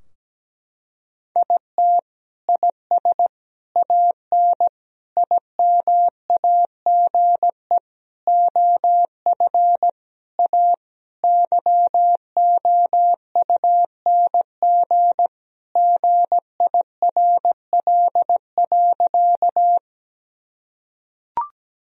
Morse code, heard as a keyed tone, about 17 wpm, 700 Hz. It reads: IT IS AN IMAGE OF A YOUNG GIRL.
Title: Sentences from Top 1000 Words - Review Part 4 - 17wpm